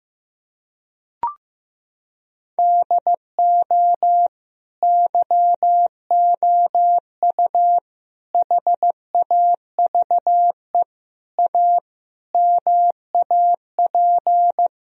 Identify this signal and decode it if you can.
Morse code, heard as a keyed tone, about 15 wpm, 700 Hz. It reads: DO YOU HAVE A MAP
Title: Sentences from Top 1000 Words - Review Part 2 - 15wpm